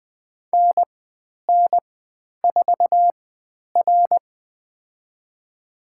Morse code, heard as a keyed tone, about 20 wpm, spaced as for 10 wpm, 700 Hz. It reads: NN4R
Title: 095 Lesson 16g - Letter c - Call Signs 20wpm